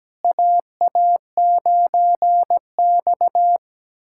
Morse code, heard as a keyed tone, about 17 wpm, 700 Hz. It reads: AA9X
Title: USA Callsign DE Callsign - 4 Characters - 17wpm